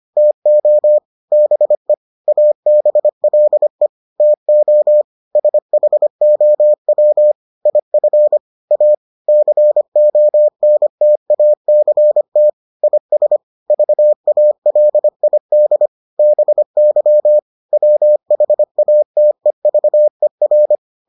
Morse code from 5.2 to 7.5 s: SHOW

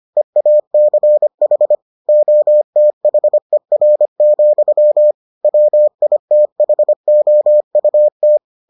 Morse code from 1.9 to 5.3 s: OTHER,

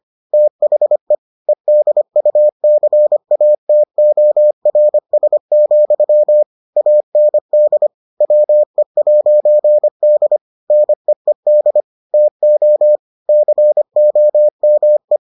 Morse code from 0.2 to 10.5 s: THE EDUCATORS, AND WE'D

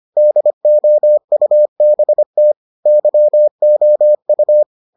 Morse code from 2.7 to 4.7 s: YOU

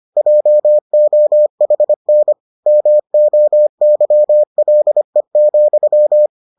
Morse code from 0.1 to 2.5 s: JOHN

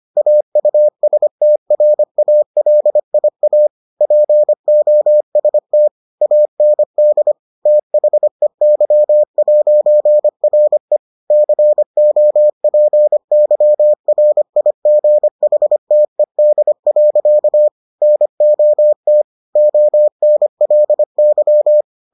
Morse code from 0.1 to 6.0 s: AUSTRALIA POST

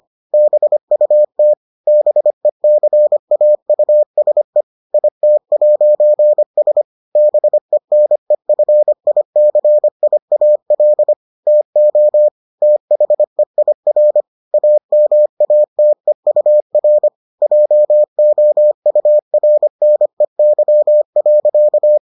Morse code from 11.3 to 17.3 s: TO THEIR AMATEUR

